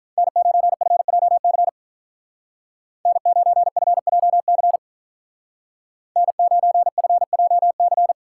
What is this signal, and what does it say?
Morse code, heard as a keyed tone, about 47 wpm, 700 Hz. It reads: N9FJC N9FJC N9FJC